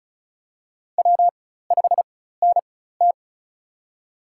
Morse code, heard as a keyed tone, about 36 wpm, spaced as for 16 wpm, 700 Hz. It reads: W5NT
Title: USA Call Signs - 4 Characters - 4x Character Spacing - 35wpm